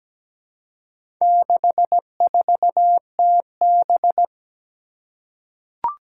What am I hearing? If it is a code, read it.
Morse code, heard as a keyed tone, about 17 wpm, 700 Hz. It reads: 64TB
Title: Four Challenging Letter-Number Combinations 17wpm